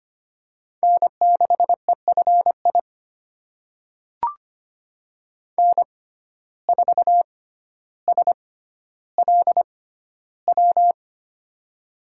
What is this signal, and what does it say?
Morse code, heard as a keyed tone, about 25 wpm, spaced as for 13 wpm, 700 Hz. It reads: N6EFI N4SLW
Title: USA Call Signs - 5 Characters - 6x Character Spacing - 25wpm